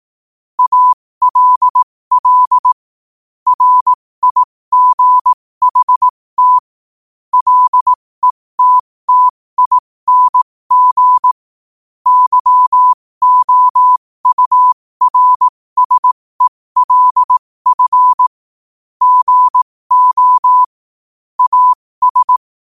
Morse code, heard as a keyed tone, about 18 wpm, spaced as for 15 wpm, 1000 Hz. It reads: ALL RIGHT LETTING YOURSELF GO AS
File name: qotdfiles / 15wpm